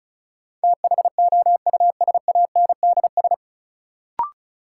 Morse code, heard as a keyed tone, about 34 wpm, 700 Hz. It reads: THOUSANDS
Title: Sets of 4 Words - New Word Drill - 35wpm